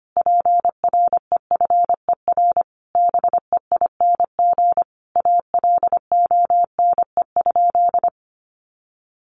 Morse code, heard as a keyed tone, about 25 wpm, 700 Hz. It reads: PREFER BEING ALONE?
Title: Sentences from Top 1100 Words 25wpm